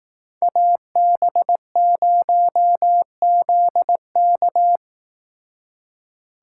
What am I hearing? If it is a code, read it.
Morse code, heard as a keyed tone, about 18 wpm, 700 Hz. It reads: AB0ZK
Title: Speed Racing - USA 2x2 Call Signs 15wpm